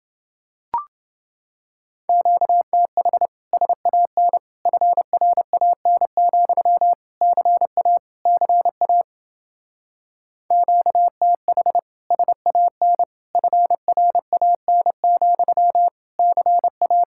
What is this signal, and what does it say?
Morse code, heard as a keyed tone, about 28 wpm, 700 Hz. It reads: QTH SAN FRAN, CA CA QTH SAN FRAN, CA